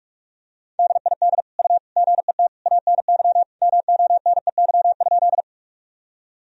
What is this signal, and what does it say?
Morse code, heard as a keyed tone, about 46 wpm, 700 Hz. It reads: DID U GET ANY MONEY?
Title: Sentences from Top 400 Words and Abbreviations 45wpm